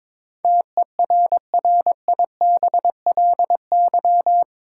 Morse code, heard as a keyed tone, about 22 wpm, 700 Hz. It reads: TERRIBLY
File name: practice / 22wpm / Eight Letter Words - 22wpm